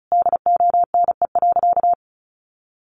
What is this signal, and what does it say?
Morse code, heard as a keyed tone, about 34 wpm, 700 Hz. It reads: DONE.
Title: Sentences from Top 800 Words 35wpm